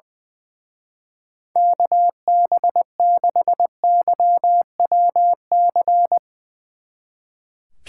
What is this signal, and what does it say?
Morse code, heard as a keyed tone, about 21 wpm, 700 Hz. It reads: KB6YWC
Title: Speed Racing - USA 2x3 Call Signs 20wpm